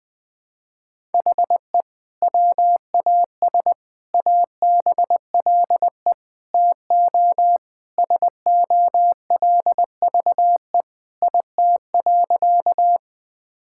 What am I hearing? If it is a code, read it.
Morse code, heard as a keyed tone, about 20 wpm, 700 Hz. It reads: HE WAS ABLE TO SOLVE IT.
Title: Sentences with 3-Word Phrases - 20wpm